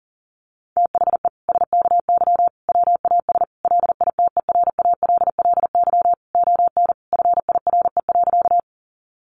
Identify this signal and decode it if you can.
Morse code, heard as a keyed tone, about 40 wpm, 700 Hz. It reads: THE SKY WAS LITERALLY ON FIRE.
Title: Sentences from Top 1400 Words 40wpm